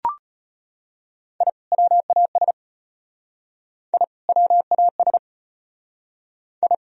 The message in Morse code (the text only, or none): I WAS I WAS I